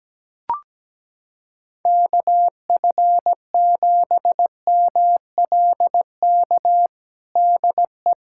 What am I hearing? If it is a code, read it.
Morse code, heard as a keyed tone, about 17 wpm, 700 Hz. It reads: KF7MLK DE
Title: USA Callsign DE Callsign 17wpm